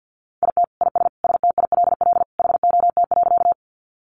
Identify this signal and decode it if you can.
Morse code, heard as a keyed tone, about 50 wpm, 700 Hz. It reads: IT IS STILL HOT.
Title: Additional Easy Phrases 50wpm